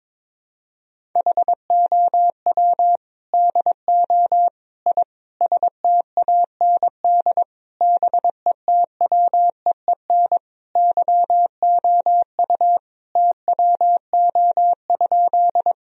Morse code, heard as a keyed tone, about 22 wpm, 700 Hz. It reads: HOW DO I STAND BETWEEN YOU TWO?
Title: Sentences from Top 1000 Words - Review Part 1 - 22wpm